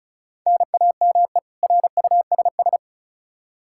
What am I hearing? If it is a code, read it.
Morse code, heard as a keyed tone, about 36 wpm, 700 Hz. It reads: NAME RUSS
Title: K1USN SST QSO Practice 35wpm